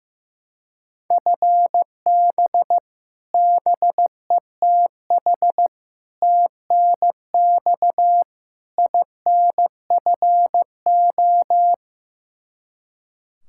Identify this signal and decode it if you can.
Morse code, heard as a keyed tone, about 15 wpm, 700 Hz. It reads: FB BETH TNX INFO
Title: Practice QSO - Exchange 3 - More Info - 15wpm